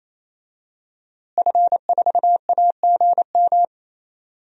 Morse code, heard as a keyed tone, about 28 wpm, 700 Hz. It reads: F4AGM
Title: DXCC-100 Call Signs - 5 Characters - Repeated 3x - 28wpm